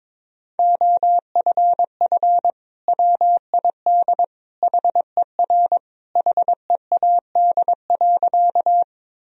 Morse code, heard as a keyed tone, about 22 wpm, 700 Hz. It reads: OFF WID HER HEAD.
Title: Sentences from Top 300 Words and Abbreviations 22wpm